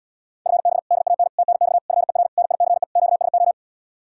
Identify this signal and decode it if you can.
Morse code, heard as a keyed tone, about 79 wpm, 700 Hz. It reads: FB DAN TNX FER NICE QSO